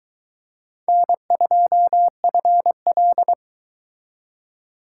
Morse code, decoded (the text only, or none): N2FL